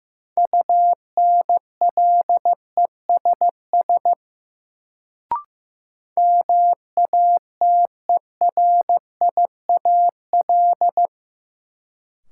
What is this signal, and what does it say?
Morse code, heard as a keyed tone, about 15 wpm, 700 Hz. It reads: UNLESS MATERIAL
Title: Sentences from Top 1000 Words - Review Part 5 - New Word Drill - 15wpm